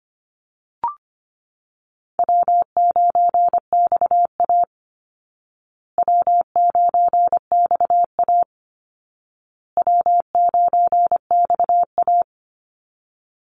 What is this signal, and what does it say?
Morse code, heard as a keyed tone, about 25 wpm, 700 Hz. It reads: W9XA W9XA W9XA
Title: Call Signs Repeated 2x - 25wpm